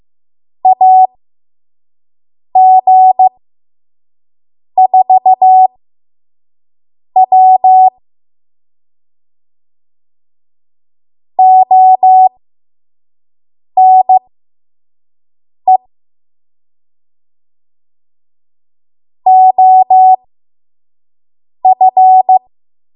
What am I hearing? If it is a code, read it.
Morse code, heard as a keyed tone, about 15 wpm, spaced as for 5 wpm, 750 Hz. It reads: AG4W ONE OF
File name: Morse / 05WPM